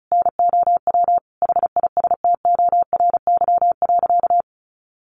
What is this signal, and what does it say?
Morse code, heard as a keyed tone, about 34 wpm, 700 Hz. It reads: NOW HISTORY.
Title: Sentences from Top 1500 Words 35wpm